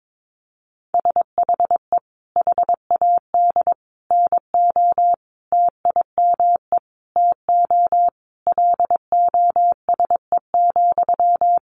Morse code, heard as a keyed tone, about 22 wpm, 700 Hz. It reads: SHE HAD NO TIME TO LOSE,
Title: Sentences from Top 1100 Words 22wpm